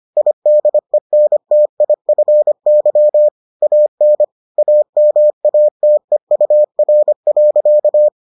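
Morse code, heard as a keyed tone, about 25 wpm, 600 Hz. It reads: IDENTIFY AN AMATEUR.
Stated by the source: Morse edition of the Foundations of Amateur Radio episodes.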